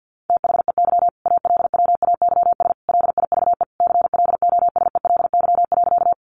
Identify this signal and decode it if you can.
Morse code, heard as a keyed tone, about 50 wpm, 700 Hz. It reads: THEY ALWAYS LIVE CLOSELY.
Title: Sentences from Top 1400 Words 50wpm